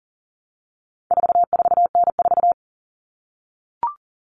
Morse code, heard as a keyed tone, about 40 wpm, 700 Hz. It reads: 44N4
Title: All Cut Numbers - 4-Digit 40wpm